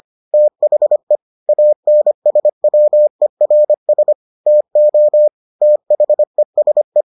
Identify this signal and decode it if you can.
Morse code, heard as a keyed tone, about 25 wpm, 600 Hz. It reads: THE ANSWERS TO THESE